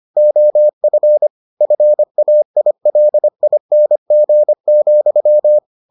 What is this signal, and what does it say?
Morse code, heard as a keyed tone, about 25 wpm, 600 Hz. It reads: OF FAILING,